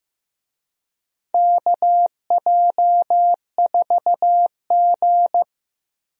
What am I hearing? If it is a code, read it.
Morse code, heard as a keyed tone, about 15 wpm, 700 Hz. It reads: KJ4G
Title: USA Call Signs - 4 Characters - 15wpm